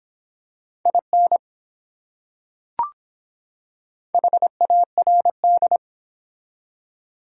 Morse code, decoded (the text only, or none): IN HARD